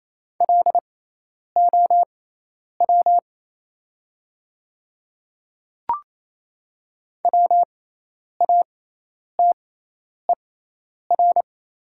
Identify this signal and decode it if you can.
Morse code, heard as a keyed tone, about 28 wpm, spaced as for 10 wpm, 700 Hz. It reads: LOW WATER
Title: 6x Character Spacing - Top 200 Words - 28wpm